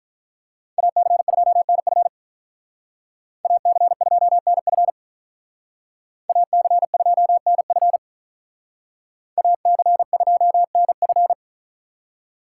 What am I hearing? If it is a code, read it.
Morse code, heard as a keyed tone, about 44 wpm, 700 Hz. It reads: AC2NF AC2NF AC2NF AC2NF